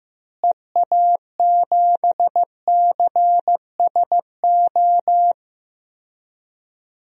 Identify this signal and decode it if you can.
Morse code, heard as a keyed tone, about 15 wpm, 700 Hz. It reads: EA7CSO
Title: DXCC-100 Call Signs - 6 Characters - Repeated 3x - 15wpm